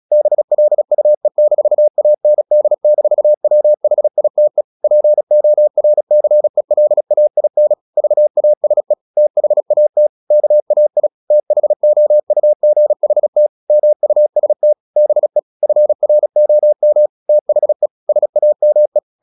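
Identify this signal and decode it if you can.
Morse code, heard as a keyed tone, about 36 wpm, 600 Hz. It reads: BLUE-AND-WHITE PORCELAIN VASE THAT KAI THOUGHT MUST BE FROM THE SAME
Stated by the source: Morse practice at 36 wpm, text of Ken Follett's novel Never, from CW med Gnister.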